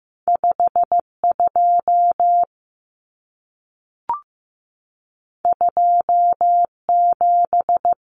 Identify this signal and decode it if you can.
Morse code, heard as a keyed tone, about 15 wpm, 700 Hz. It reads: 52 27
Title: Numbers - 2-Digit 15wpm